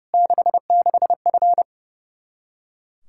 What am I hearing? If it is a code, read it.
Morse code, heard as a keyed tone, about 30 wpm, 700 Hz. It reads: 66F
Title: Practice QSO - Exchange 2 - Info - 30wpm